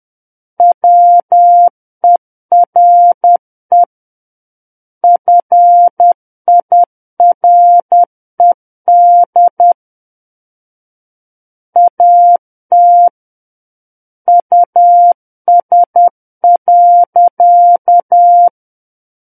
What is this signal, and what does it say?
Morse code, heard as a keyed tone, about 10 wpm, 700 Hz. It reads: WERE FIRED AT US.
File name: Book One / 01-Chapter 1 - 10 WPM